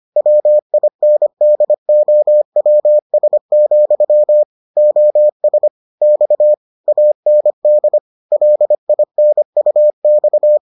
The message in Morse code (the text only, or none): WINDOWS, OS X AND LINUX